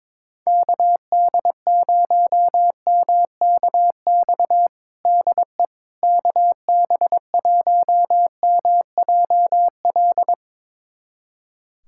Morse code, heard as a keyed tone, about 22 wpm, 700 Hz. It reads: KD0MKX DE KB1MJL
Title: USA Callsign DE Callsign - 6 Characters - 22wpm